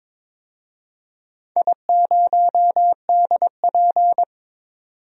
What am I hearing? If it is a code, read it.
Morse code, heard as a keyed tone, about 22 wpm, 700 Hz. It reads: I0DP
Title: DXCC-100 Call Signs - 4 Characters - 22wpm